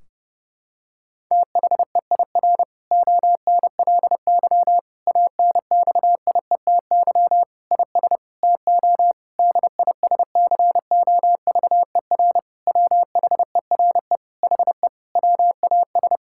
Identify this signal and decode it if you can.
Morse code, heard as a keyed tone, about 30 wpm, 700 Hz. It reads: THEIR ONLY ANXIETY IS TO DISCOVER WHERE HE WAS
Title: Sentences from Top 1600 Words 30wpm